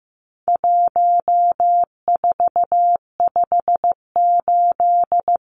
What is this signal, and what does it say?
Morse code, heard as a keyed tone, about 15 wpm, 700 Hz. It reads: 1458
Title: Numbers - 4-Digit 15wpm